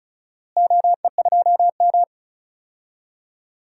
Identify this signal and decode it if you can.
Morse code, heard as a keyed tone, about 34 wpm, 700 Hz. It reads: OE2M
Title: DXCC-100 Call Signs - 4 Characters - Repeated 3x - 35wpm